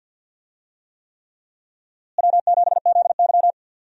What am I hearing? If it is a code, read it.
Morse code, heard as a keyed tone, about 45 wpm, 700 Hz. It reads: W7ZY
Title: USA Call Signs - 4 Characters - Repeated 3x - 50wpm